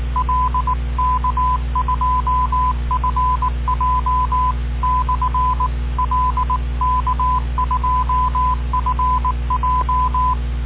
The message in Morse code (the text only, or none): L K2FJ/L K2FJ